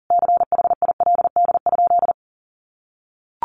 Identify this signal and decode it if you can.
Morse code, heard as a keyed tone, about 40 wpm, 700 Hz. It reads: CHILD?
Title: Sentences from Top 800 Words 40wpm